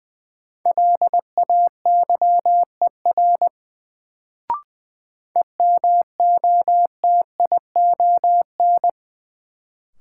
Morse code, heard as a keyed tone, about 21 wpm, 700 Hz. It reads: LAYER EMOTION
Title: Sentences from Top 1600 Words - New Word Drill Alt - 20wpm